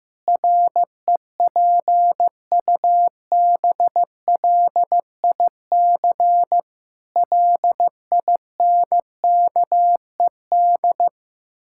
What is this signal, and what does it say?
Morse code, heard as a keyed tone, about 15 wpm, 700 Hz. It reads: REPUBLIC LINKED
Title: Pangram Challenge v2 - 15wpm